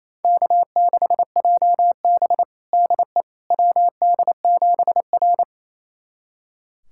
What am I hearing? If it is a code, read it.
Morse code, heard as a keyed tone, about 28 wpm, 690 Hz. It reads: K6JB DE WD7R
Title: USA Callsign DE Callsign - 4 Characters - 28wpm